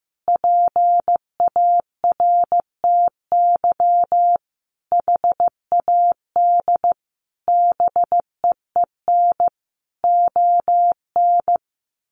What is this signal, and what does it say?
Morse code, heard as a keyed tone, about 15 wpm, 700 Hz. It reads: PARTY HAD BEEN ON
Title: Sentences from Top 1400 Words 15wpm